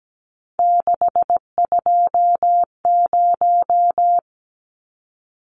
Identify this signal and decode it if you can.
Morse code, heard as a keyed tone, about 17 wpm, 700 Hz. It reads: 620
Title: Numbers - 3-Digit 17wpm